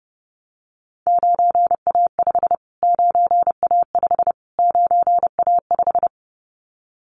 Morse code, heard as a keyed tone, about 30 wpm, 700 Hz. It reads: 9A5 9A5 9A5
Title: Callsign Prefixes for Top 25 DXCC - Repeated 3x - 30wpm